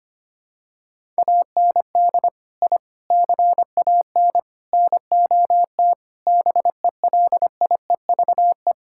Morse code, heard as a keyed tone, about 25 wpm, 700 Hz. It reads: AND I CAN NOT BELIEVE
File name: practice / 25wpm / Sentences with 4-Word Phrases - 25wpm